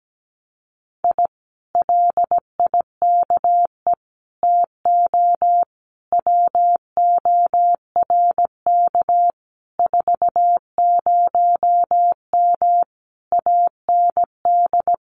Morse code, heard as a keyed tone, about 17 wpm, 700 Hz. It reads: I LIKE TO WORK 40M AND
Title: Common QSO Phrases Encore 17wpm